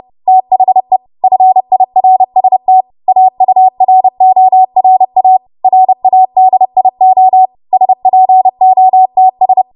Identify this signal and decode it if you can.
Morse code, heard as a keyed tone, about 31 wpm, 750 Hz. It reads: THE FIRST AURORA RADIO SPOTS
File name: Morse / 30WPM